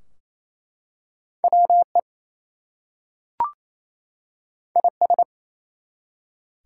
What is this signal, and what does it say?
Morse code, heard as a keyed tone, about 27 wpm, 700 Hz. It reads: WE IS
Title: Two Letter Words - 28wpm